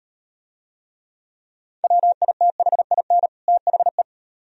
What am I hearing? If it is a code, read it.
Morse code, heard as a keyed tone, about 35 wpm, 700 Hz. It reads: WITHIN THE